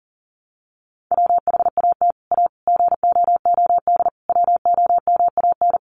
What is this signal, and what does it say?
Morse code, heard as a keyed tone, about 40 wpm, 700 Hz. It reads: WHAT A GOOD WOMAN
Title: Sentences from Top 1100 Words 40wpm